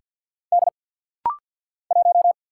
Morse code, heard as a keyed tone, about 50 wpm, 700 Hz. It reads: D 1
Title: Single Letter-Number - Rapid-Fire 50wpm